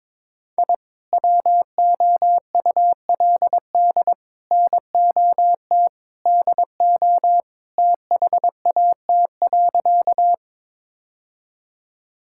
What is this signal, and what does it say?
Morse code, heard as a keyed tone, about 21 wpm, 700 Hz. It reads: I WOULD NOT DO THAT.
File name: practice / 22wpm / Sentences with 3-Word Phrases - 22wpm